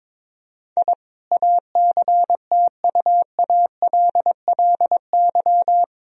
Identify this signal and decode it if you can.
Morse code, heard as a keyed tone, about 22 wpm, 700 Hz. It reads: I ACTUALLY
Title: Sentences from Top 1000 Words - Review Part 4 - 22wpm